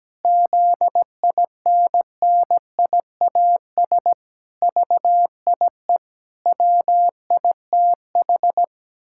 Morse code, heard as a keyed tone, about 17 wpm, 700 Hz. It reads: ZINNIAS VIE WITH
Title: Pangram Challenge v2 - 17wpm